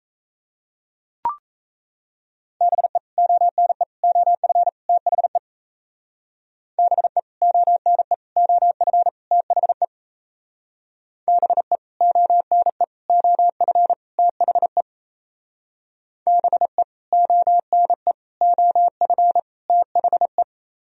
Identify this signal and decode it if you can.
Morse code, heard as a keyed tone, about 32 wpm, 700 Hz. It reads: BE ONE OF THE BE ONE OF THE BE ONE OF THE BE ONE OF THE
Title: Speed-Racing - Sets of 4 Words 28wpm